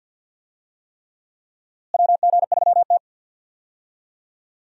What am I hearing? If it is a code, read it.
Morse code, heard as a keyed tone, about 51 wpm, 700 Hz. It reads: WG3T
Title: USA Call Signs - 4 Characters - Repeated 3x - 50wpm